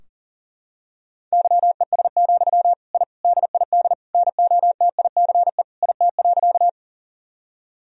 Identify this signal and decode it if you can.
Morse code, heard as a keyed tone, about 40 wpm, 700 Hz. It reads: YES, I DID NOTICE IT.
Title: Sentences from Top 1000 Words - Review Part 2 - 40wpm